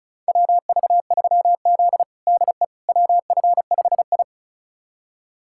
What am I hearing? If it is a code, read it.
Morse code, heard as a keyed tone, about 34 wpm, 700 Hz. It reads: WV3Z DE WF5I
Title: USA Callsign DE Callsign - 4 Characters - 35wpm